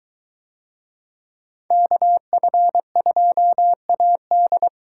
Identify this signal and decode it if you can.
Morse code, heard as a keyed tone, about 22 wpm, 700 Hz. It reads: KF2AD